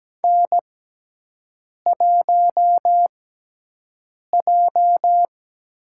Morse code, heard as a keyed tone, about 17 wpm, spaced as for 6 wpm, 700 Hz. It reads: N1J
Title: USA Call Signs - 6x Character Spacing - 17wpm